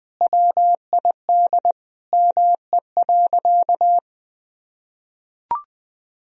Morse code, heard as a keyed tone, about 21 wpm, 700 Hz. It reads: WID ME.
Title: Sentences from Top 300 Words and Abbreviations 20wpm